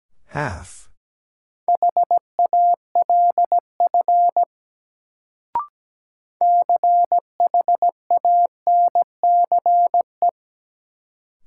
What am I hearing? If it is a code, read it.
Morse code, heard as a keyed tone, about 17 wpm, 700 Hz. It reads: HALF CHANCE
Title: Sets of 3 Words - New Word Drill - 17wpm